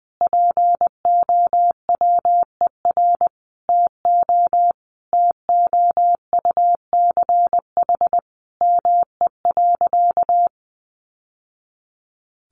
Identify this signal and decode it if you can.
Morse code, heard as a keyed tone, about 20 wpm, 700 Hz. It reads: POWER TO TOUCH ME.
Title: Sentences from Top 1200 Words 20wpm